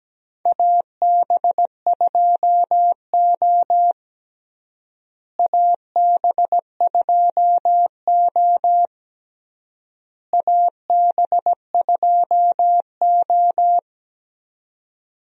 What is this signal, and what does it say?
Morse code, heard as a keyed tone, about 17 wpm, 700 Hz. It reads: AB2O AB2O AB2O
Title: USA Call Signs - 4 Characters - Repeated 3x - 17wpm